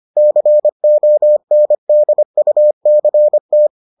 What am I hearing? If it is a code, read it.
Morse code, heard as a keyed tone, about 25 wpm, 600 Hz. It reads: CONDUCT